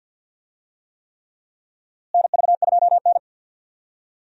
Morse code, heard as a keyed tone, about 50 wpm, 700 Hz. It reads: NV2N
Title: USA Call Signs - 4 Characters - Repeated 3x - 50wpm